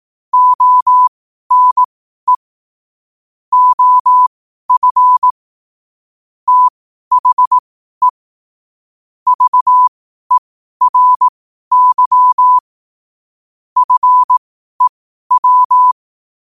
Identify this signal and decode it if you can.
Morse code, heard as a keyed tone, about 18 wpm, spaced as for 12 wpm, 1000 Hz. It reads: ONE OF THE VERY FEW
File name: qotdfiles / 12wpm